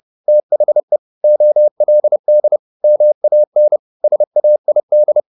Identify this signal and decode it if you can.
Morse code, heard as a keyed tone, about 30 wpm, 600 Hz. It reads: THE OLD MAN SAID
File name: the old man and the sea / the old man and the sea 1 30wpm